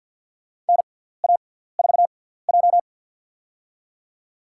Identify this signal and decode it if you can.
Morse code, heard as a keyed tone, about 52 wpm, spaced as for 17 wpm, 700 Hz. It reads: NA4J
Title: USA Call Signs - 4 Characters - 6x Character Spacing - 50wpm